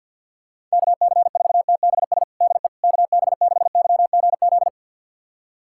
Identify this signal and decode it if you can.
Morse code, heard as a keyed tone, about 47 wpm, 700 Hz. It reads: KC4TBS DE KB6YGZ
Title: USA Callsign DE Callsign - 6 Characters - 50wpm